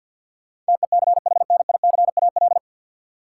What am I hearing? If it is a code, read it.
Morse code, heard as a keyed tone, about 53 wpm, 700 Hz. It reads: TECHNICAL
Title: Sentences from Top 1100 Words - New Word Drill Alt - 50wpm